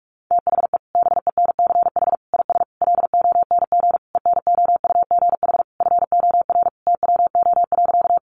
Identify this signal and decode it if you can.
Morse code, heard as a keyed tone, about 44 wpm, 700 Hz. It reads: THE BENCH IS LONG ENOUGH FOR TWO.
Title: Sentences from Top 1400 Words 45wpm